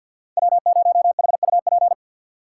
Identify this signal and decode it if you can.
Morse code, heard as a keyed tone, about 51 wpm, 700 Hz. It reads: W0HUP
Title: USA Call Signs - 3x Character Spacing - 50wpm